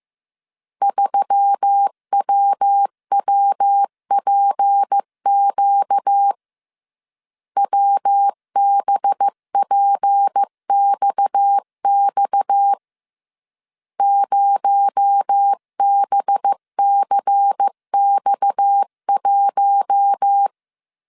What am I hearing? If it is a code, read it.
Morse code, heard as a keyed tone, about 15 wpm, 800 Hz. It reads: 3WWPQ WBPXX 0BCX1